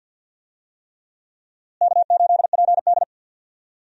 Morse code, heard as a keyed tone, about 51 wpm, 700 Hz. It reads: K8PD
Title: USA Call Signs - 4 Characters - Repeated 3x - 50wpm